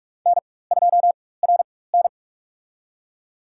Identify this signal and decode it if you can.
Morse code, heard as a keyed tone, about 46 wpm, spaced as for 21 wpm, 700 Hz. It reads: N2RN
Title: USA Call Signs - 4 Characters - 4x Character Spacing - 45wpm